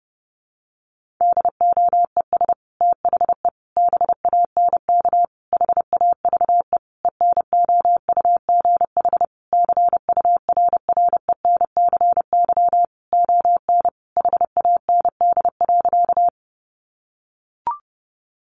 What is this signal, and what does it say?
Morse code, heard as a keyed tone, about 30 wpm, 700 Hz. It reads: DOES THE BANK HAVE ENOUGH CURRENCY ON HAND.
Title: Sentences from Top 1500 Words 30wpm